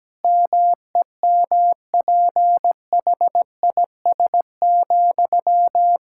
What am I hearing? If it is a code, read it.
Morse code, heard as a keyed tone, about 17 wpm, 700 Hz. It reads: MEMPHIS,
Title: Practice QSO - Exchange 1 - Beginning - 17wpm